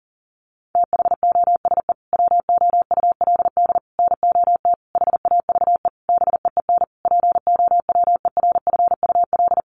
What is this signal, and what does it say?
Morse code, heard as a keyed tone, about 40 wpm, 700 Hz. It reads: THOSE WOULD NOT HAVE BEEN POWERFUL
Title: Sentences from Top 1300 Words 40wpm